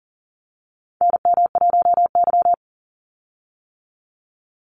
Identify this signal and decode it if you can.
Morse code, heard as a keyed tone, about 40 wpm, 700 Hz. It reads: NM1Y
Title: Call Signs - 40wpm